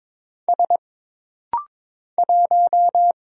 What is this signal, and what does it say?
Morse code, heard as a keyed tone, about 22 wpm, 700 Hz. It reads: S 1
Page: This rapid-fire timing is geared to push your ICR (Instant Character Recognition) skills.